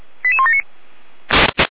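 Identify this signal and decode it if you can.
Five-tone selective call (ZVEI), heard as quick runs of five tones, tones of about 70 ms. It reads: 88179
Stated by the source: ZVEI five-tone call 88171 received on an SDR.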